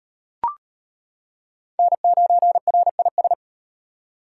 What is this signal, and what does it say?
Morse code, heard as a keyed tone, about 38 wpm, 700 Hz. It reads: N9RIS